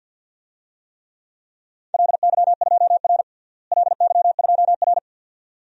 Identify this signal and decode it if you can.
Morse code, heard as a keyed tone, about 46 wpm, 700 Hz. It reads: LY2R LY2R
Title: DXCC-100 Call Signs - 4 Characters - Repeated 2x - 50wpm